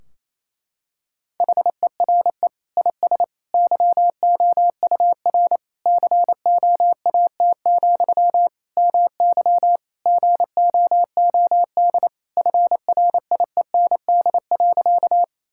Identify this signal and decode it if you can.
Morse code, heard as a keyed tone, about 28 wpm, 700 Hz. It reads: HERE IS YOUR COAT, MY GOOD FRIEND.
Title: Sentences from Top 1600 Words 28wpm